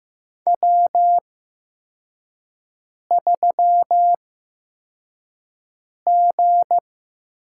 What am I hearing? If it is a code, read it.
Morse code, heard as a keyed tone, about 15 wpm, spaced as for 4 wpm, 700 Hz. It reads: W3G
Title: USA Call Signs - 4 Characters - 8x Character Spacing - 15wpm